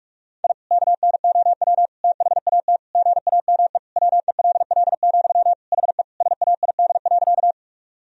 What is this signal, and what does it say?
Morse code, heard as a keyed tone, about 46 wpm, 700 Hz. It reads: I KNOW THAT GAME WELL, HE SAID.
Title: Sentences from Top 1000 Words - Review Part 3 - 45wpm